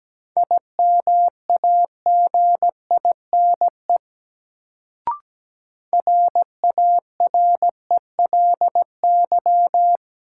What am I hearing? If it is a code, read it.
Morse code, heard as a keyed tone, about 17 wpm, 700 Hz. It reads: IMAGINE RARELY
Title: Sentences from Top 1400 Words - New Word Drill - 17wpm